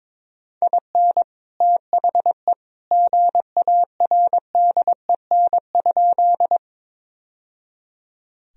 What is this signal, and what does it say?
Morse code, heard as a keyed tone, about 22 wpm, 700 Hz. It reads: IN THE GARDEN?
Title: Sentences from Top 1000 Words - Review Part 4 - 22wpm